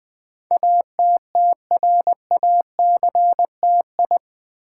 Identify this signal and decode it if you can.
Morse code, heard as a keyed tone, about 20 wpm, 700 Hz. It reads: ATTRACTI
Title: Build A Word - 4 Letters At A Time - 10-Letter Words - 20wpm